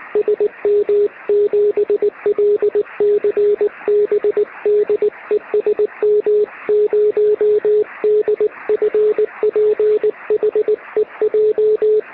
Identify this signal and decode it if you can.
Morse code, heard as a keyed tone, about 20 wpm, 440 Hz. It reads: SM7LCB DE SM0DFP HEJ
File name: sm7lcb / 2047z sm0dfp 13cm